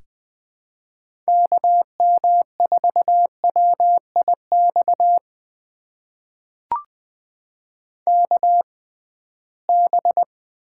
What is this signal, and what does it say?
Morse code, heard as a keyed tone, about 20 wpm, spaced as for 15 wpm, 700 Hz. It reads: KM4WIX KB
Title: USA Call Signs - 6 Characters - 6x Character Spacing - 20wpm